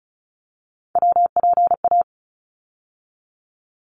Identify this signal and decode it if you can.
Morse code, heard as a keyed tone, about 36 wpm, 700 Hz. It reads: WPA
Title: ARRL Sections 35wpm